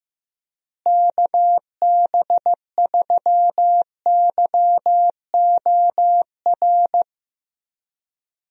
Morse code, heard as a keyed tone, about 15 wpm, 700 Hz. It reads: KB3YOR
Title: USA Call Signs - 6 Characters - Repeated 2x - 15wpm